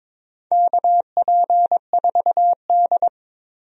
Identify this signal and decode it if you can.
Morse code, heard as a keyed tone, about 22 wpm, 700 Hz. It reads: KP4D
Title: DXCC-100 Call Signs - 4 Characters - 22wpm